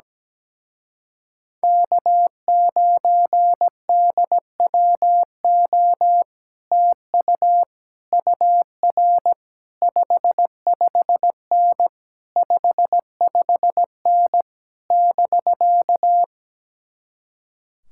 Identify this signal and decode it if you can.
Morse code, heard as a keyed tone, about 17 wpm, 700 Hz. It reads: K9DWO TU UR 55N 55N <BK>
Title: Speed-Racing Common QSO Phrases - 17wpm